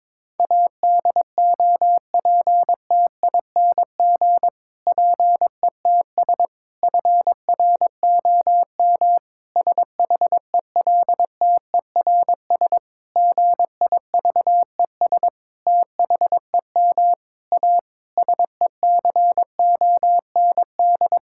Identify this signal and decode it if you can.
Morse code, heard as a keyed tone, about 22 wpm, 700 Hz. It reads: ADOPTING PETS FROM SHELTERS GIVES THEM A SECOND
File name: Difficult Sentences with Hints - 22wpm